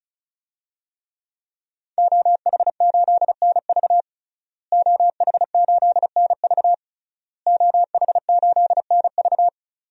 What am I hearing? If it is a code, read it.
Morse code, heard as a keyed tone, about 36 wpm, 700 Hz. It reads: OH8NV OH8NV OH8NV
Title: DXCC-100 Call Signs - 5 Characters - Repeated 3x - 35wpm